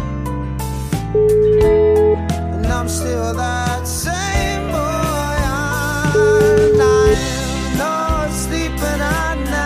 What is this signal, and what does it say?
Ringback tone (Europe, 425 Hz), with noise.